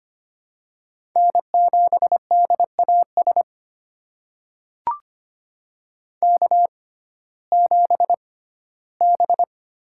Morse code, heard as a keyed tone, about 25 wpm, spaced as for 15 wpm, 700 Hz. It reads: N7DAS K7B
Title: USA Call Signs - 5 Characters - 6x Character Spacing - 25wpm